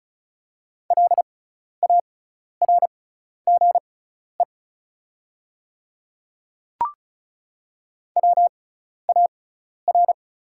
Morse code, heard as a keyed tone, about 34 wpm, spaced as for 12 wpm, 690 Hz. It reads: LARGE WAR
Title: 6x Character Spacing - Top 500 Words - 35wpm